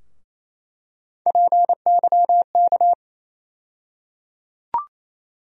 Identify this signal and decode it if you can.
Morse code, heard as a keyed tone, about 28 wpm, 700 Hz. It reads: PYK
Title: USA Call Sign Postfixes - 3 Letters 28wpm